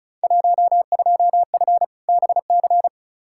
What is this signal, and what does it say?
Morse code, heard as a keyed tone, about 36 wpm, 700 Hz. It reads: 12F BC
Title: Field Day Exchanges Encore - 35wpm